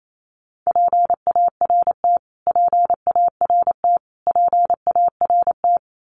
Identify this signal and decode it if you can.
Morse code, heard as a keyed tone, about 28 wpm, 700 Hz. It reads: PART PART PART
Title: Top 500 Words Repeated 28wpm